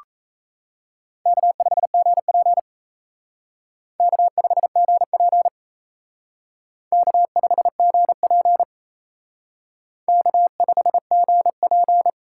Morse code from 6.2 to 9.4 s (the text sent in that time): K5GP